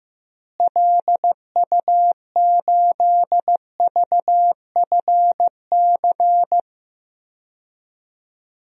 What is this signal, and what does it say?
Morse code, heard as a keyed tone, about 15 wpm, 700 Hz. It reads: LU8VFC
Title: DXCC-100 Call Signs - 6 Characters - Repeated 3x - 15wpm